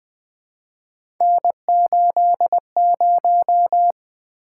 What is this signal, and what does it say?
Morse code, heard as a keyed tone, about 20 wpm, 700 Hz. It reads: N80
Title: Three Letter-Number Combinations - 20wpm